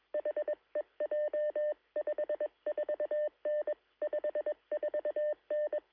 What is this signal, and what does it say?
Morse code, heard as a keyed tone, about 21 wpm, 590 Hz. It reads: HEJ 54N 54N
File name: sm7lcb / 1652z sp4mpb 13cm